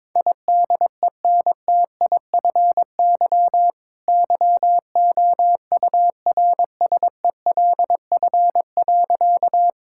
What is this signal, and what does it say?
Morse code, heard as a keyed tone, about 22 wpm, 700 Hz. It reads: IDENTIFY YOURSELF.
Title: Sentences from Top 1000 Words - Review Part 6 - 22wpm